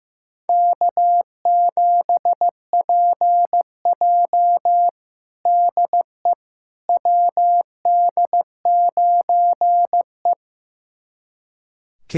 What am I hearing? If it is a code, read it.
Morse code, heard as a keyed tone, about 15 wpm, 700 Hz. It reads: K7PJ DE WD9E
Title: USA Callsign DE Callsign - 4 Characters - 15wpm